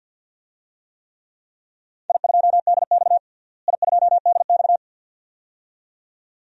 Morse code, heard as a keyed tone, about 52 wpm, 700 Hz. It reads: I2DX I2DX
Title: DXCC-100 Call Signs - 4 Characters - Repeated 2x - 50wpm